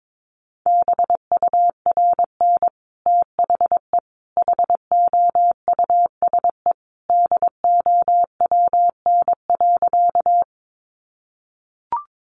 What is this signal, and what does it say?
Morse code, heard as a keyed tone, about 22 wpm, 700 Hz. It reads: BURN THE HOUSE DOWN.
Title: Sentences from Top 1300 Words 22wpm